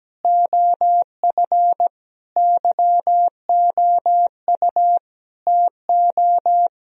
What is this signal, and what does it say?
Morse code, heard as a keyed tone, about 17 wpm, 700 Hz. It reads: OF YOU TO
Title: Sentences from Top 1000 Words - Review Part 1 - 17wpm